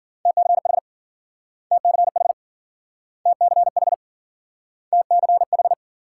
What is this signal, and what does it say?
Morse code, heard as a keyed tone, about 53 wpm, 700 Hz. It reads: TCH TCH TCH TCH